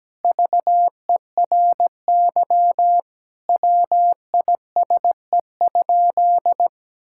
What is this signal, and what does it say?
Morse code, heard as a keyed tone, about 17 wpm, 700 Hz. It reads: VERY WISE?
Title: Sentences from Top 1600 Words 17wpm